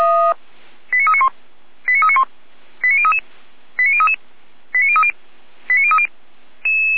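Five-tone selective call (ZVEI), heard as quick runs of five tones, tones of about 70 ms. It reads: 89391 89391 89033 89033 89030 89030
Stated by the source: ZVEI five-tone selective call 89050 received by SDR.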